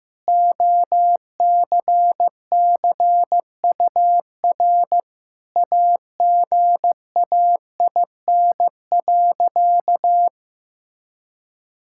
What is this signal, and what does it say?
Morse code, heard as a keyed tone, about 16 wpm, 700 Hz. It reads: OCCUR AGAIN.
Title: Sentences from Top 1000 Words - Review Part 6 - 15wpm